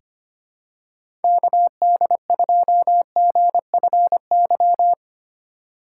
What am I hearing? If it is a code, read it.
Morse code, heard as a keyed tone, about 25 wpm, 700 Hz. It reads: KD2GFY
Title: USA Call Signs - 6 Characters - 4x Character Spacing - 25wpm